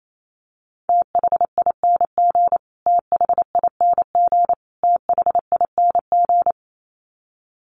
Morse code, heard as a keyed tone, about 28 wpm, 700 Hz. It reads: THING THING THING
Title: Top 300 Words Repeated 28wpm